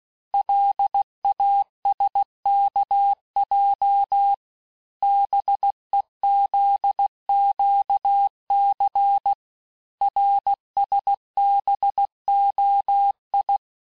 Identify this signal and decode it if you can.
Morse code, heard as a keyed tone, about 16 wpm, 800 Hz. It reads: LASKJ BEZQC RSBOI
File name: cours-du-lundi files / cours-1592--teste-chiffre-vitesse-15mpm